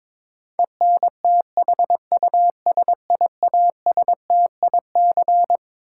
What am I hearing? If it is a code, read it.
Morse code, heard as a keyed tone, about 21 wpm, 700 Hz. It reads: ENTHUSIASTIC
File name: Twelve Letter Words - 22wpm